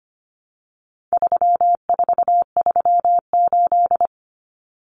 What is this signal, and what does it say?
Morse code, heard as a keyed tone, about 25 wpm, 700 Hz. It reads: 3438
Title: Numbers - 4-Digit 25wpm